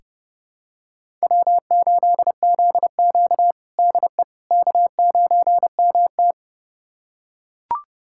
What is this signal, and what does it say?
Morse code, heard as a keyed tone, about 30 wpm, 700 Hz. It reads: W8ZQ DE K9MT
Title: USA Callsign DE Callsign - 4 Characters - 30wpm